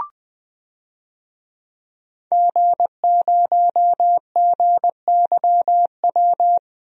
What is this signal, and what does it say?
Morse code, heard as a keyed tone, about 20 wpm, 700 Hz. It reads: G0GYW
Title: DXCC-100 Call Signs - 5 Characters - 20wpm